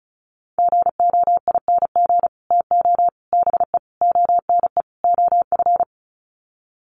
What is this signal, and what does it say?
Morse code, heard as a keyed tone, about 34 wpm, 700 Hz. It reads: GOING TO BE ONE OF
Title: Sets of 5 Words Encore2 35wpm